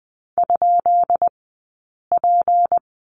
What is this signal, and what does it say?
Morse code, heard as a keyed tone, about 20 wpm, 700 Hz. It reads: ? P
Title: 190 Lesson 31d - Letter v - Two Characters 20wpm